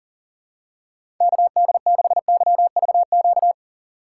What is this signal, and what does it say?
Morse code, heard as a keyed tone, about 40 wpm, 700 Hz. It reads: KD6YVQ
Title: USA Call Signs - 6 Characters - 4x Character Spacing - 40wpm